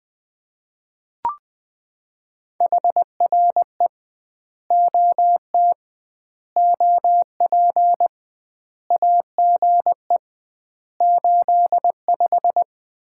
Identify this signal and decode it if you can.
Morse code, heard as a keyed tone, about 20 wpm, 700 Hz. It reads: HRE OT OP AGE 85
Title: Common QSO Phrases - 2x Word-Spacing - 20wpm